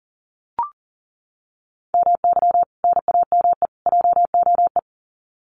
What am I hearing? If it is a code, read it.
Morse code, heard as a keyed tone, about 40 wpm, 700 Hz. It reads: MY NAME JOE
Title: Common QSO Phrases 40wpm